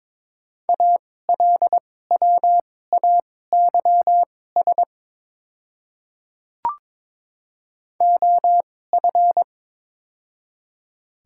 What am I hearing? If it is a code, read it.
Morse code, heard as a keyed tone, about 22 wpm, spaced as for 16 wpm, 700 Hz. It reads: ALWAYS OF
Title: 2x Character Spacing - Top 300 Words - 22wpm